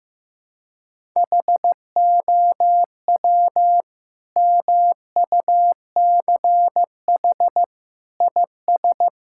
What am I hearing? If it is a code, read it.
Morse code, heard as a keyed tone, about 16 wpm, 700 Hz. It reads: HOW MUCH IS
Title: Sentences from Top 1000 Words - Review Part 6 - 15wpm